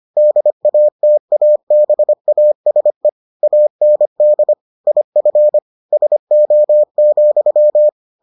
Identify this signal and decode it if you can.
Morse code, heard as a keyed tone, about 25 wpm, 600 Hz. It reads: DATABASE AND IF SO,